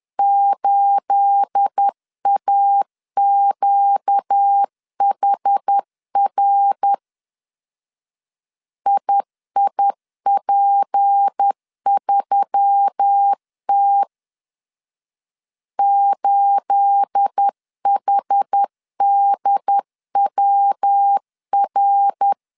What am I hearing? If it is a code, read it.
Morse code, heard as a keyed tone, about 10 wpm, 800 Hz. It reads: 8AQHR IIP3T 8HDWR
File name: CW / DET / 1 9 1500C S A 10wpm 4511